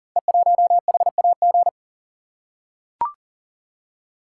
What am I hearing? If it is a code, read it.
Morse code, heard as a keyed tone, about 39 wpm, 700 Hz. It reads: E1HAG
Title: Code Groups - 5 Letter:Numbers - 40wpm